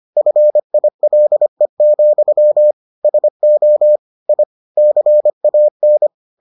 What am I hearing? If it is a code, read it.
Morse code, heard as a keyed tone, about 25 wpm, 600 Hz. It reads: FILE, SO I CAN